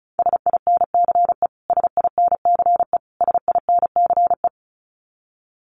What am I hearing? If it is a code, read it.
Morse code, heard as a keyed tone, about 34 wpm, 700 Hz. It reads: SINCE SINCE SINCE
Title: Top 500 Words Repeated 35wpm